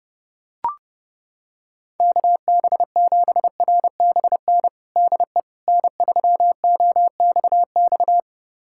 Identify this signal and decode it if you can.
Morse code, heard as a keyed tone, about 32 wpm, 700 Hz. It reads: KB7RBN DE N3OXX
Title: USA Callsign DE Callsign 30wpm